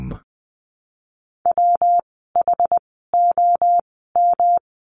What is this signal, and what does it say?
Morse code, heard as a keyed tone, about 20 wpm, spaced as for 14 wpm, 700 Hz. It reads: WHOM
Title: 115 Lesson 19f - Letter w - Words 20wpm